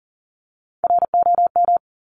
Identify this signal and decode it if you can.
Morse code, heard as a keyed tone, about 39 wpm, 700 Hz. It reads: ROM
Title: Top Three Letter Combinations in Words 40wpm